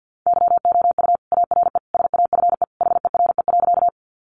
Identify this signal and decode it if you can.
Morse code, heard as a keyed tone, about 50 wpm, 700 Hz. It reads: YOU ARE SAFE HERE.
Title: Sentences from Top 900 Words 50wpm